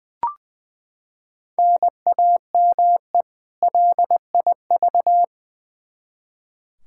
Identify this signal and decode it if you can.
Morse code, heard as a keyed tone, about 20 wpm, 700 Hz. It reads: NAME LIV
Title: Practice QSO - Exchange 1 - Beginning - 20wpm